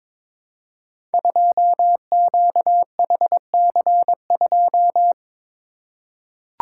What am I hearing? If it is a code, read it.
Morse code, heard as a keyed tone, about 22 wpm, 700 Hz. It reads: 2QHC2
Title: Code Groups - 5 Letter:Numbers - 22wpm